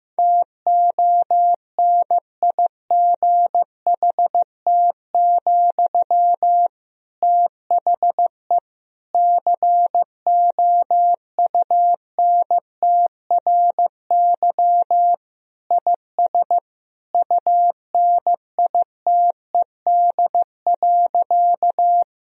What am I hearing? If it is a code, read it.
Morse code, heard as a keyed tone, about 16 wpm, 700 Hz. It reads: TONIGHT, THE COUNTRY IS UNITED.
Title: Sentences from Top 1000 Words - Review Part 4 - 15wpm